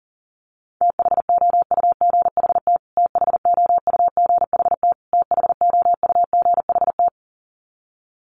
Morse code, heard as a keyed tone, about 40 wpm, 700 Hz. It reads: THOUGHT THOUGHT THOUGHT
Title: Top 300 Words Repeated 40wpm